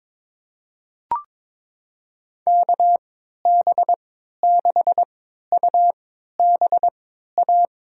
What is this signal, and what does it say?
Morse code, heard as a keyed tone, about 22 wpm, spaced as for 13 wpm, 700 Hz. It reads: KB6UBA
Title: USA Call Signs - 6 Characters - 3x Character Spacing - 22wpm